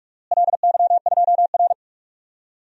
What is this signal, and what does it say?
Morse code, heard as a keyed tone, about 46 wpm, 700 Hz. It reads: LY2R
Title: DXCC-100 Call Signs - 4 Characters - Repeated 2x - 45wpm